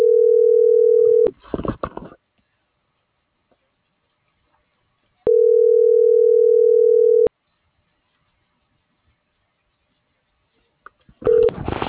Ringback tone (North America), 440 + 480 Hz.